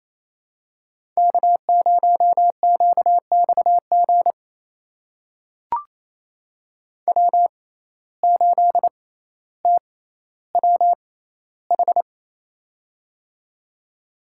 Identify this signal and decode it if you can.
Morse code, heard as a keyed tone, about 28 wpm, spaced as for 14 wpm, 700 Hz. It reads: K0QXG W8TWH
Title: USA Call Signs - 5 Characters - 6x Character Spacing - 28wpm